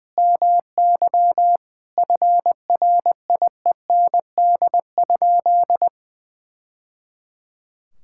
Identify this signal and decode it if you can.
Morse code, heard as a keyed tone, about 20 wpm, 700 Hz. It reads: MY FRIEND?
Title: Sentences from Top 1000 Words - Review Part 5 - 20wpm